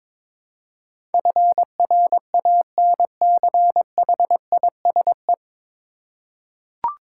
Morse code, heard as a keyed tone, about 22 wpm, 700 Hz. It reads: FRANCHISE
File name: practice / 22wpm / 9 Letter Words - First 3 Letters Given - 22wpm